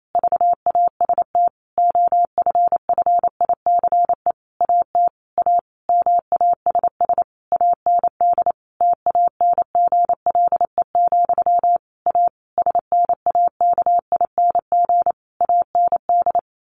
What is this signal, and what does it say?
Morse code, heard as a keyed tone, about 28 wpm, 700 Hz. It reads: VAST OFFICE AT A MASS AND TANGLE, A SNAKING AND
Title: A Sound Of Thunder 28wpm